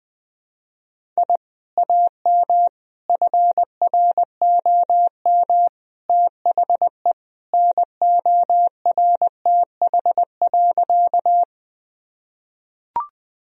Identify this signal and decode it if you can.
Morse code, heard as a keyed tone, about 20 wpm, 700 Hz. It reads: I AM FROM THE NORTH.
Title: Sentences from Top 1000 Words - Review Part 1 - 20wpm